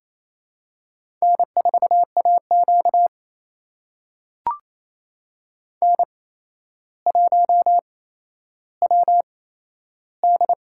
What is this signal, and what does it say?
Morse code, heard as a keyed tone, about 28 wpm, spaced as for 12 wpm, 700 Hz. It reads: N4AQ N1WD
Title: USA Call Signs - 4 Characters - 8x Character Spacing - 28wpm